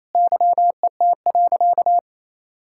Morse code, heard as a keyed tone, about 28 wpm, 700 Hz. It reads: YET.
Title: Sentences from Top 1000 Words - Review Part 6 - 28wpm